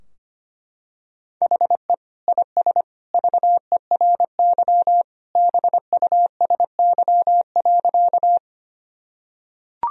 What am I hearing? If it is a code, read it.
Morse code, heard as a keyed tone, about 25 wpm, 700 Hz. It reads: HE IS VERY BUSY.
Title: Sentences from Top 1000 Words - Review Part 3 - 25wpm